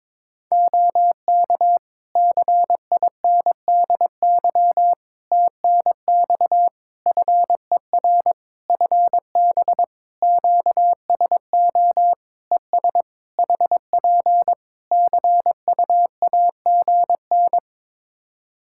Morse code, heard as a keyed tone, about 22 wpm, 700 Hz. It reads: OK CINDY TNX FER FB QSO ES HP CUAGN
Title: K1USN SST QSO Practice 22wpm